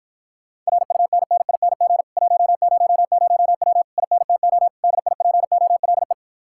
Morse code, heard as a keyed tone, about 46 wpm, 700 Hz. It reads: RUNNING 100W INTO DIPOLE